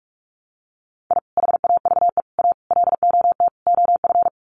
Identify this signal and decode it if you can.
Morse code, heard as a keyed tone, about 44 wpm, 700 Hz. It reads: I HAVE A LOT OF
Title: Sets of 5 Words 45wpm